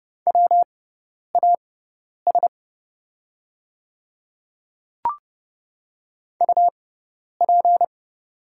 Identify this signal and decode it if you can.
Morse code, heard as a keyed tone, about 30 wpm, spaced as for 10 wpm, 700 Hz. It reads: WAS UP
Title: 6x Character Spacing - Top 100 Words - 30wpm